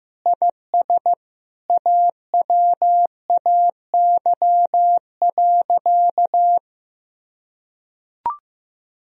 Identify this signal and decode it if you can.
Morse code, heard as a keyed tone, about 15 wpm, 700 Hz. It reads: IS AWAY.
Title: Sentences from Top 1000 Words - Review Part 2 - 15wpm